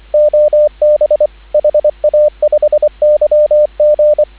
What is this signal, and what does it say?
Morse code, heard as a keyed tone, about 25 wpm, 590 Hz, with radio noise. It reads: OB HA5YG